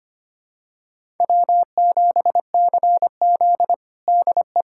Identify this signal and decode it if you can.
Morse code, heard as a keyed tone, about 25 wpm, 700 Hz. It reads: W7CZ DE
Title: USA Callsign DE Callsign - 4 Characters - 25wpm